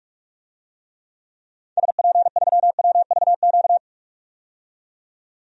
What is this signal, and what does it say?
Morse code, heard as a keyed tone, about 46 wpm, 700 Hz. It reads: SP3WVQ
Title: DXCC-100 Call Signs - 6 Characters - 45wpm